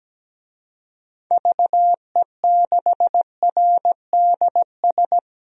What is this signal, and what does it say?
Morse code, heard as a keyed tone, about 17 wpm, 700 Hz. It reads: VE6RDS
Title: DXCC-100 Call Signs - 6 Characters - 17wpm